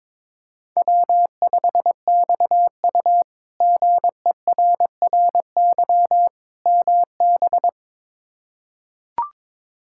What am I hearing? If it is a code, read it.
Morse code, heard as a keyed tone, about 23 wpm, 700 Hz. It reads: W5XU GERRY MB
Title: K1USN SST QSO Practice - 22wpm